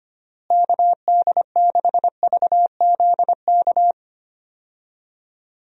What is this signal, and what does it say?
Morse code, heard as a keyed tone, about 25 wpm, 700 Hz. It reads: KD6VZK
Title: USA Call Signs - 6 Characters - Repeated 3x - 25wpm